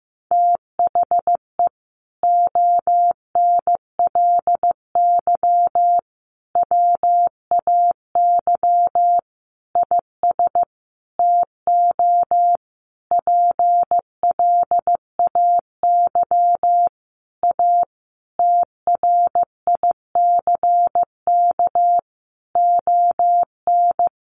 Morse code, heard as a keyed tone, about 15 wpm, 700 Hz. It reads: THE ONLY WAY IS TO PLAY A TRICK ON
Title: Sentences from Top 1500 Words 15wpm